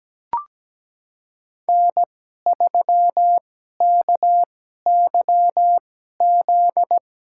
Morse code, heard as a keyed tone, about 17 wpm, spaced as for 12 wpm, 700 Hz. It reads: N3KYZ
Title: USA Call Signs - 5 Characters - 2x Character Spacing - 17wpm